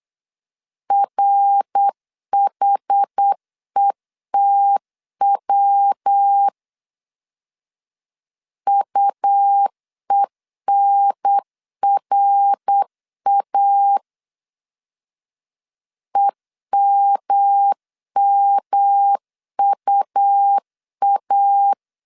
Morse code, heard as a keyed tone, about 8 wpm, 800 Hz. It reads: RHETW UENRA EMMUA